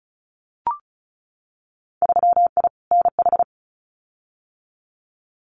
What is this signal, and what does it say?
Morse code, heard as a keyed tone, about 34 wpm, 700 Hz. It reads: 3I NH
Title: Winter Field Day Exchanges Encore 35wpm